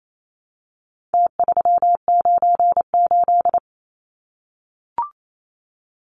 Morse code, heard as a keyed tone, about 28 wpm, 700 Hz. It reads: T398
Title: Numbers - Contest Style - 4-Digit 28wpm